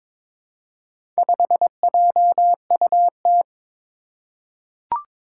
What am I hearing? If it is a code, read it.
Morse code, heard as a keyed tone, about 22 wpm, 700 Hz. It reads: 5JUT
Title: Four Challenging Letter-Number Combinations 22wpm